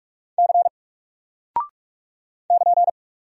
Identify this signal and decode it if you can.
Morse code, heard as a keyed tone, about 44 wpm, 700 Hz. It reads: C <KN>
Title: Single Character (Letter:Number:Punctuation:Prosign) 45wpm